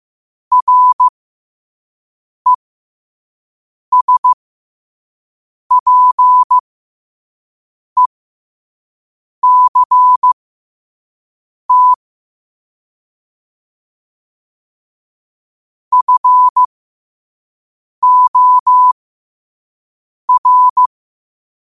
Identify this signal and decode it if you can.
Morse code, heard as a keyed tone, about 15 wpm, spaced as for 5 wpm, 1000 Hz. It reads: RESPECT FOR
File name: qotdfiles / 5wpm